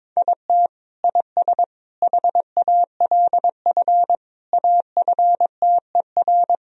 Morse code, heard as a keyed tone, about 22 wpm, 700 Hz. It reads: IT IS HALF AFTER
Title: Sentences from Top 1000 Words - Review Part 1 - 22wpm